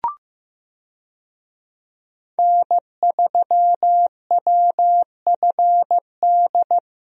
Morse code, heard as a keyed tone, about 15 wpm, 700 Hz. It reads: N3WFD
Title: USA Call Signs - 5 Characters - Repeated 2x - 15wpm